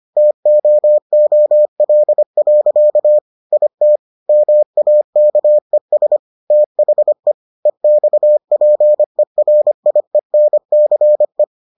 Morse code, heard as a keyed tone, about 25 wpm, 600 Hz. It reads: TOOL. IT MAKES THE EXPERIENCE